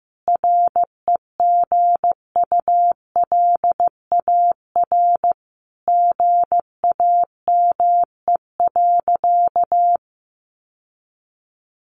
Morse code, heard as a keyed tone, about 15 wpm, 700 Hz. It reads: REGULAR GAME.
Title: Sentences from Top 900 Words 15wpm